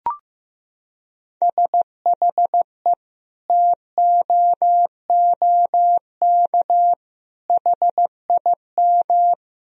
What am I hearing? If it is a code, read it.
Morse code, heard as a keyed tone, about 16 wpm, 700 Hz. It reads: SHE TOOK HIM
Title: Sentences from Top 1000 Words - Review Part 4 - 15wpm